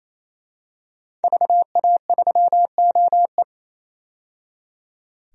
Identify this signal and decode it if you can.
Morse code, heard as a keyed tone, about 28 wpm, 700 Hz. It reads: VA3OE
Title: DXCC-100 Call Signs - 5 Characters - 28wpm